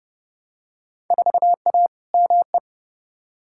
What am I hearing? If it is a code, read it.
Morse code, heard as a keyed tone, about 30 wpm, 700 Hz. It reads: 4A ME
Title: Field Day Exchanges - 30wpm